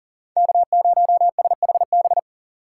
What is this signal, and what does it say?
Morse code, heard as a keyed tone, about 53 wpm, 700 Hz. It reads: K0SHB